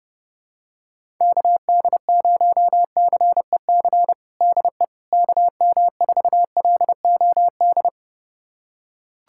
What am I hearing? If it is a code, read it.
Morse code, heard as a keyed tone, about 30 wpm, 700 Hz. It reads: KD0CEC DE KM4LOD
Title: USA Callsign DE Callsign - 6 Characters - 30wpm